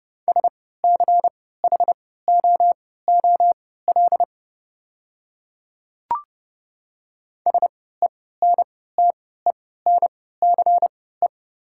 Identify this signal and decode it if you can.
Morse code, heard as a keyed tone, about 30 wpm, spaced as for 17 wpm, 700 Hz. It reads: SCHOOL SENTENCE
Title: 3x Character Spacing - Top 300 Words - 30wpm